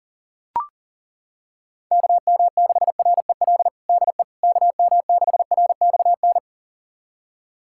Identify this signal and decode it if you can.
Morse code, heard as a keyed tone, about 40 wpm, 700 Hz. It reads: KM6REL DE KM6RXN
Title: USA Callsign DE Callsign - 6 Characters - 40wpm